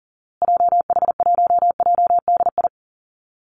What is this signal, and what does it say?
Morse code, heard as a keyed tone, about 40 wpm, 700 Hz. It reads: JH1JDI
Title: Call Signs Repeated 2x - 40wpm